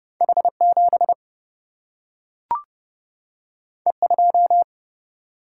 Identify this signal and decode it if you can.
Morse code, heard as a keyed tone, about 30 wpm, 700 Hz. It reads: H7 E2
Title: Single-Letter Single-Number 30wpm